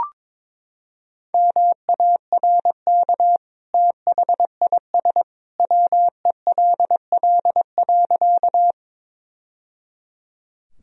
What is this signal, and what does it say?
Morse code, heard as a keyed tone, about 22 wpm, 700 Hz. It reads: MARK THIS WELL.
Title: Sentences from Top 1000 Words - Review Part 1 - 22wpm